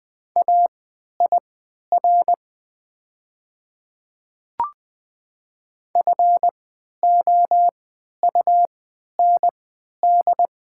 Morse code, heard as a keyed tone, about 20 wpm, spaced as for 11 wpm, 700 Hz. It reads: AIR FOUND
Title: 3x Character Spacing - Top 300 Words - 20wpm